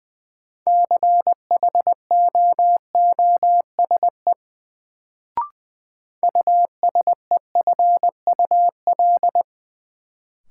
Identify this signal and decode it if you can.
Morse code, heard as a keyed tone, about 20 wpm, 700 Hz. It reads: CHOOSE USEFUL
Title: Sentences from Top 700 Words - New Word Drill - 20wpm